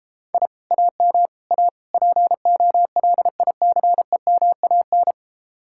Morse code, heard as a keyed tone, about 29 wpm, 700 Hz. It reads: I AM A POLICEMAN